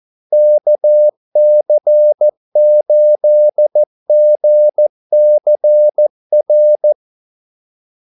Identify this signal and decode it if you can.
Morse code, heard as a keyed tone, about 14 wpm, 600 Hz. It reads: KC8GCR
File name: Callsigns-14wpm